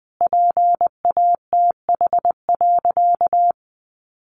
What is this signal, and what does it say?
Morse code, heard as a keyed tone, about 20 wpm, 700 Hz. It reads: PATH.
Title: A Sound Of Thunder 20wpm